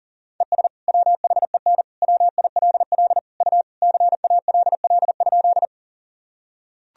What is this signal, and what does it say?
Morse code, heard as a keyed tone, about 37 wpm, 700 Hz. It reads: ES WHEN WILL U CALL?
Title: Sentences from Top 100 Words and Abbreviations 40wpm